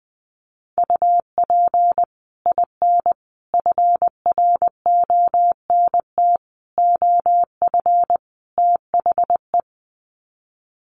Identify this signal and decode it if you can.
Morse code, heard as a keyed tone, about 20 wpm, 700 Hz. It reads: UP IN FRONT OF THE
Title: Sets of 5 Words Encore2 20wpm